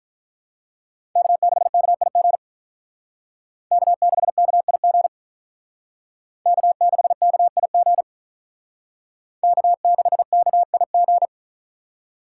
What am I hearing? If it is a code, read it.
Morse code, heard as a keyed tone, about 46 wpm, 700 Hz. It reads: K6KIG K6KIG K6KIG K6KIG